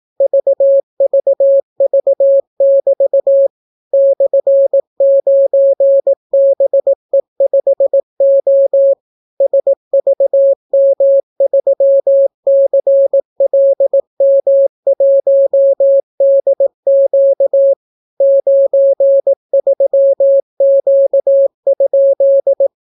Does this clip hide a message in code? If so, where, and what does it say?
Yes: Morse code from 0.2 to 22.7 s, heard as a keyed tone, about 18 wpm, 550 Hz. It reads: VVV<BT> /9BE5O SVM3CLM1DQ 93Q?